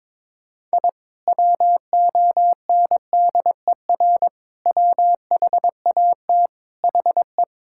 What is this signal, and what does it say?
Morse code, heard as a keyed tone, about 22 wpm, 700 Hz. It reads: I WONDER WHAT HE
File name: practice / 22wpm / Sentences with 2-Word Phrases - 22wpm